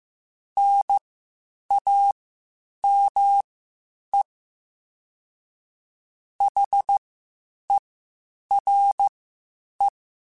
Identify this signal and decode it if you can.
Morse code, heard as a keyed tone, about 15 wpm, spaced as for 8 wpm, 800 Hz. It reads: NAME HERE
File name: (spaced) / QSOs Spaced-mp3s / kc7haa @13 sp